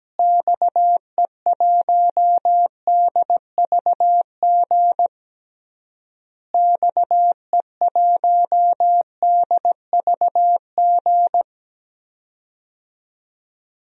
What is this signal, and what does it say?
Morse code, heard as a keyed tone, about 17 wpm, 700 Hz. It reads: XE1DVG XE1DVG
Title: DXCC-100 Call Signs - 6 Characters - Repeated 3x - 17wpm